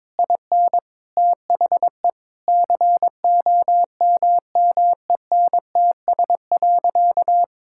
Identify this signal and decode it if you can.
Morse code, heard as a keyed tone, about 21 wpm, 700 Hz. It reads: IN THE COMMENTS.
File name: practice / 22wpm / Sentences with 8-Word Phrases - 22wpm